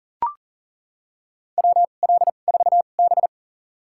Morse code, heard as a keyed tone, about 40 wpm, spaced as for 29 wpm, 700 Hz. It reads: WL4B
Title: USA Call Signs - 4 Characters - 2x Character Spacing - 40wpm